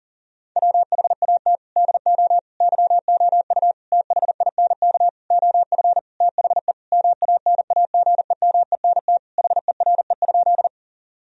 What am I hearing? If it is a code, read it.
Morse code, heard as a keyed tone, about 40 wpm, 700 Hz. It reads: WHAT DO YOU THINK OF THE MANAGEMENT HERE?
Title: Sentences from Top 1000 Words - Review Part 4 - 40wpm